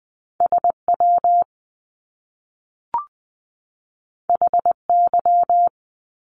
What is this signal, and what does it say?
Morse code, heard as a keyed tone, about 20 wpm, 700 Hz. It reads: SW HY
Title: Top Two Letter Combinations in Words 20wpm